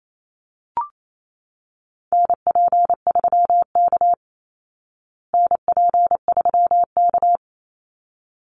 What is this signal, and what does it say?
Morse code, heard as a keyed tone, about 28 wpm, 700 Hz. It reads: NP3K NP3K
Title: Call Signs Repeated 3x - 28wpm